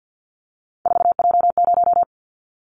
Morse code, heard as a keyed tone, about 50 wpm, 700 Hz. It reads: 4J0
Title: Callsign Prefixes for Top 100 DXCC - 50wpm